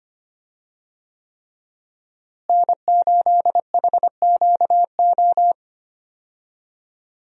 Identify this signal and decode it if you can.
Morse code, heard as a keyed tone, about 25 wpm, 700 Hz. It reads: N8HQO
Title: USA Call Signs - 5 Characters - Repeated 3x - 25wpm